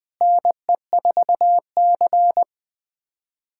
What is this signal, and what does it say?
Morse code, heard as a keyed tone, about 20 wpm, 700 Hz. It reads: NE4C
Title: USA Callsign DE Callsign - 4 Characters - 20wpm